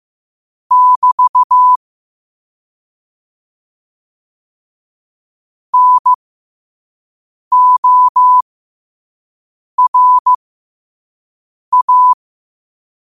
Morse code, heard as a keyed tone, about 15 wpm, spaced as for 5 wpm, 1000 Hz. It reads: <BT> NORA